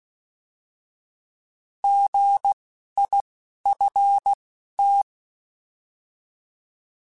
Morse code, heard as a keyed tone, about 16 wpm, 800 Hz. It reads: G I F T
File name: Misc / Misc-mp3s / Misc @10